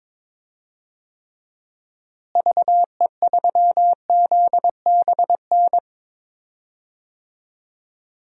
Morse code, heard as a keyed tone, about 22 wpm, 700 Hz. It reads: VE3ZBN
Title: DXCC-100 Call Signs - 6 Characters - Repeated 2x - 22wpm